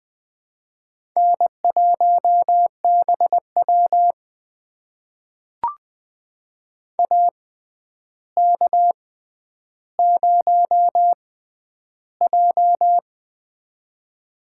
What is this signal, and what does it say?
Morse code, heard as a keyed tone, about 21 wpm, spaced as for 10 wpm, 700 Hz. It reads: N1BW AK0J
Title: USA Call Signs - 4 Characters - 6x Character Spacing - 20wpm